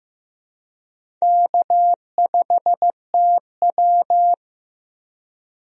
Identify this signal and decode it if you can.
Morse code, heard as a keyed tone, about 15 wpm, 700 Hz. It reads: K5TW
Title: USA Call Signs - 4 Characters - Repeated 3x - 15wpm